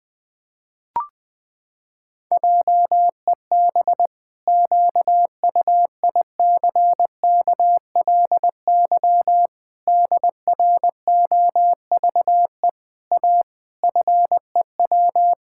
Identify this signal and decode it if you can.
Morse code, heard as a keyed tone, about 21 wpm, 700 Hz. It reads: JEB QUICKLY DROVE A FEW
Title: Pangram Challenge v2 - 20wpm